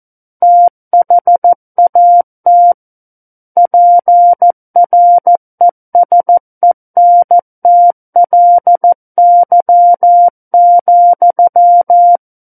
Morse code, heard as a keyed tone, about 14 wpm, 700 Hz. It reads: THAT PRESENTLY,
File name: Book One / 05-Chapter 5 - 14 WPM